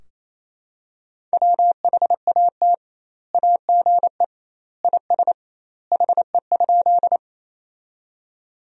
Morse code, heard as a keyed tone, about 28 wpm, 700 Hz. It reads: WHAT AGE IS HE?
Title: 2x Word-Spacing - Sentences from Top 500 Words - 28wpm